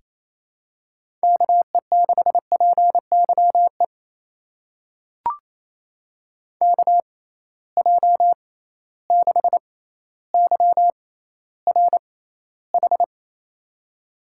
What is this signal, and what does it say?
Morse code, heard as a keyed tone, about 28 wpm, spaced as for 14 wpm, 700 Hz. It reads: KE6PYE KJ6YRH
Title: USA Call Signs - 6 Characters - 6x Character Spacing - 28wpm